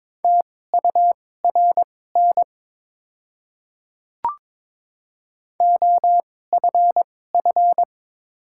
Morse code, heard as a keyed tone, about 22 wpm, spaced as for 16 wpm, 700 Hz. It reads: TURN OFF
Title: 2x Character Spacing - Top 200 Words - 22wpm